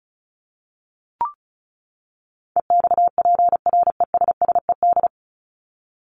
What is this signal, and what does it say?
Morse code, heard as a keyed tone, about 34 wpm, 700 Hz. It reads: EXPRESSED
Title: ED Suffix - Top 100 Words with Base Words 35wpm